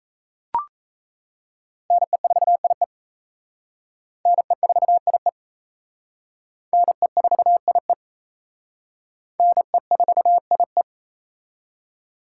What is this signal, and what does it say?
Morse code, heard as a keyed tone, about 36 wpm, 700 Hz. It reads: NE4IE NE4IE NE4IE NE4IE